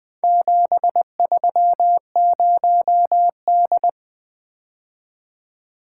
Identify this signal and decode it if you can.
Morse code, heard as a keyed tone, about 20 wpm, 700 Hz. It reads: 730D
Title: Four Letter-Number Combinations - 20wpm